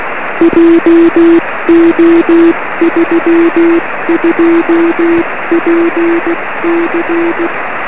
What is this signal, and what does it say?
Morse code, heard as a keyed tone, about 16 wpm, 340 Hz, with noise. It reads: JO32PC
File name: 1908z df1vb-p 13cm